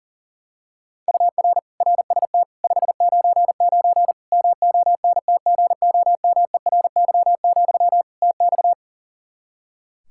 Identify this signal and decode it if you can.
Morse code, heard as a keyed tone, about 40 wpm, 700 Hz. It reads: UR RST 599 MONTGOMERY, TX
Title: K1USN SST QSO Practice 40wpm